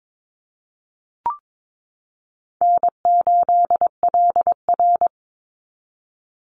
Morse code, heard as a keyed tone, about 22 wpm, 700 Hz. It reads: N8LR
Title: Easy Call Signs Repeated 3x - 22wpm